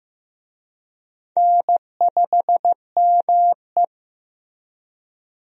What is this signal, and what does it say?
Morse code, heard as a keyed tone, about 15 wpm, 700 Hz. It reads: N5ME
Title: USA Call Signs - 4 Characters - Repeated 3x - 15wpm